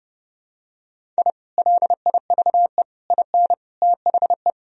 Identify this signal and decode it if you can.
Morse code, heard as a keyed tone, about 32 wpm, 700 Hz. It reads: I LIVE IN THE
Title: Sentences from Top 1000 Words - Review Part 1 - 30wpm